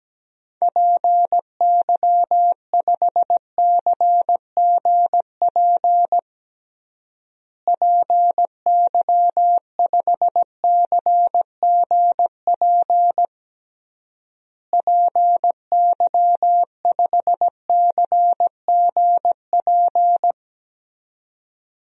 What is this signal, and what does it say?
Morse code, heard as a keyed tone, about 17 wpm, 700 Hz. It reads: PY5CGP PY5CGP PY5CGP
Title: DXCC-100 Call Signs - 6 Characters - Repeated 3x - 17wpm